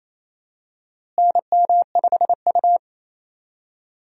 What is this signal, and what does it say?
Morse code, heard as a keyed tone, about 28 wpm, 700 Hz. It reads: NM5U
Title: USA Call Signs - 4 Characters - Repeated 3x - 28wpm